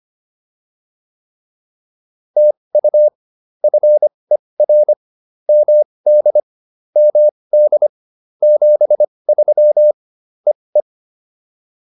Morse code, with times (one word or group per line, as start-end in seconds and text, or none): 2.4-3.1 TU
3.6-4.9 FER
5.5-6.4 MD
7.0-7.9 MD
8.4-9.9 73
10.5-10.8 EE